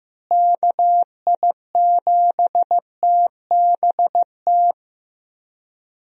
Morse code, heard as a keyed tone, about 15 wpm, 700 Hz. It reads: KI7TBT
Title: USA Call Signs - 6 Characters - Repeated 3x - 15wpm